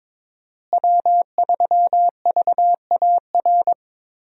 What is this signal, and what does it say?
Morse code, heard as a keyed tone, about 21 wpm, 700 Hz. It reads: W3VAR
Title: USA Call Signs - 5 Characters - Repeated 3x - 22wpm